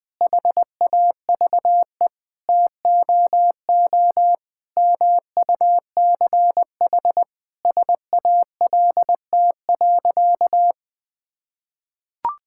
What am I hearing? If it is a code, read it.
Morse code, heard as a keyed tone, about 19 wpm, 700 Hz. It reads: HAVE TOO MUCH SALT.
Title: Sentences from Top 1000 Words - Review Part 5 - 20wpm